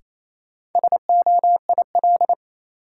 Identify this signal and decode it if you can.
Morse code, heard as a keyed tone, about 28 wpm, 700 Hz. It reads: SOIL
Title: Sentences from Top 1000 Words - Review Part 4 - New Word Drill - 28wpm